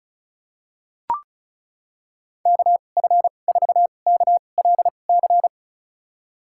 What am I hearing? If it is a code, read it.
Morse code, heard as a keyed tone, about 36 wpm, spaced as for 25 wpm, 690 Hz. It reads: KF4KLC
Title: USA Call Signs - 6 Characters - 2x Character Spacing - 35wpm